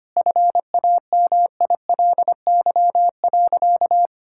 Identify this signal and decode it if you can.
Morse code, heard as a keyed tone, about 25 wpm, 700 Hz. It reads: FAMILY.
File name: practice / 25wpm / Difficult Sentences with Hints - 25wpm